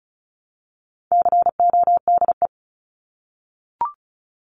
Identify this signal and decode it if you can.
Morse code, heard as a keyed tone, about 34 wpm, 700 Hz. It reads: CODE
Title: Top 1501-1750 Words 35wpm